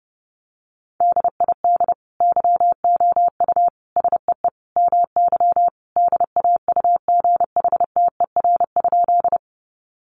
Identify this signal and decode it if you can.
Morse code, heard as a keyed tone, about 30 wpm, 700 Hz. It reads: DID YOU SEE MY DAUGHTER?
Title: Sentences from Top 1500 Words 30wpm